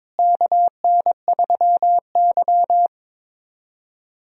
Morse code, heard as a keyed tone, about 22 wpm, 700 Hz. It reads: KN3Y
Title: USA Call Signs - 4 Characters - Repeated 3x - 22wpm